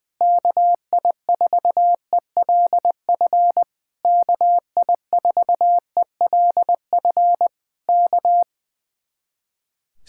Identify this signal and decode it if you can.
Morse code, heard as a keyed tone, about 21 wpm, 700 Hz. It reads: KI4ELF KI4ELF K
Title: Speed-Racing Common QSO Phrases - 20wpm